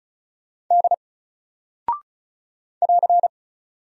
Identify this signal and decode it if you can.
Morse code, heard as a keyed tone, about 36 wpm, 700 Hz. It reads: D <AR>
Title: Single Character (Letter:Number:Punctuation:Prosign) Alt 35wpm